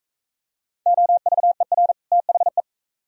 Morse code, heard as a keyed tone, about 41 wpm, 700 Hz. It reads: OVER THE